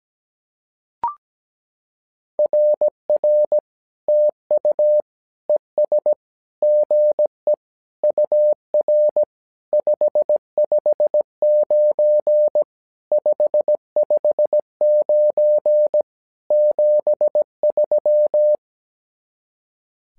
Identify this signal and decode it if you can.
Morse code, heard as a keyed tone, about 17 wpm, 600 Hz. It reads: RR TU ES GE UR 559 559 73
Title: SOTA QSO Practice 17wpm